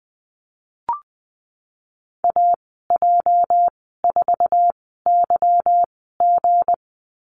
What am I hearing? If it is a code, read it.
Morse code, heard as a keyed tone, about 20 wpm, spaced as for 15 wpm, 700 Hz. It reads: AJ4YG
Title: 219 Lesson 35g - Number 0 - Call Signs 20wpm